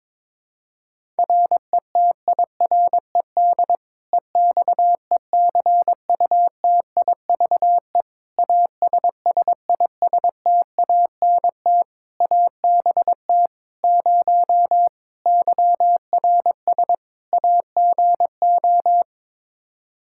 Morse code, heard as a keyed tone, about 22 wpm, 700 Hz. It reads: RETIRED EXECUTIVE ASSISTANT ABT 0 YRS AGO
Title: K1USN SST QSO Practice 22wpm